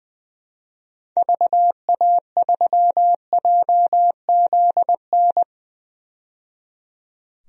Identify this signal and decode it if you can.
Morse code, heard as a keyed tone, about 21 wpm, 700 Hz. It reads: VA3JZN
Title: DXCC-100 Call Signs - 6 Characters - 20wpm